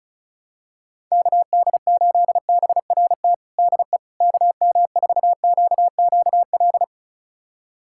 Morse code, heard as a keyed tone, about 34 wpm, 700 Hz. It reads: KD8BRT DE KM4QQL
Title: USA Callsign DE Callsign 35wpm